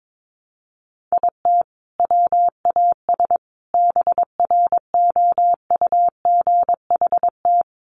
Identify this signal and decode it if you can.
Morse code, heard as a keyed tone, about 22 wpm, 700 Hz. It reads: IT WAS BROUGHT
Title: Sentences from Top 500 Words 22wpm